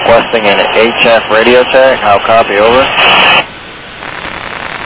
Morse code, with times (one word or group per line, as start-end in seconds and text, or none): none